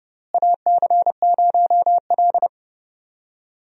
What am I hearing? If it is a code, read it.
Morse code, heard as a keyed tone, about 30 wpm, 700 Hz. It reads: AC0L
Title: USA Call Signs - 4 Characters - Repeated 3x - 30wpm